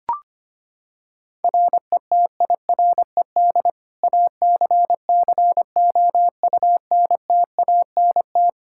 Morse code, heard as a keyed tone, about 25 wpm, 700 Hz. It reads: RETIRED ACCOUNTANT
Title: K1USN SST QSO Practice 25wpm